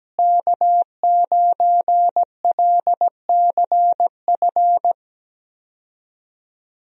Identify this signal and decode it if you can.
Morse code, heard as a keyed tone, about 17 wpm, 700 Hz. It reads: K9LCF
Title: USA Call Signs - 5 Characters - Repeated 2x - 17wpm